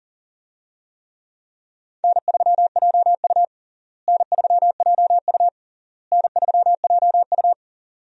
Morse code, heard as a keyed tone, about 37 wpm, 700 Hz. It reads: N3JU N3JU N3JU
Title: USA Call Signs - 4 Characters - Repeated 3x - 40wpm